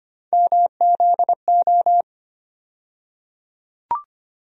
Morse code, heard as a keyed tone, about 25 wpm, 700 Hz. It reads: MZO
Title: USA Call Sign Postfixes - 3 Letters 25wpm